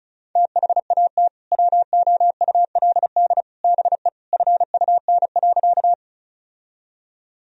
Morse code, heard as a keyed tone, about 36 wpm, 700 Hz. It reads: THAT WOULD BE FUN.
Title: Sentences from Top 1000 Words - Review Part 4 - 35wpm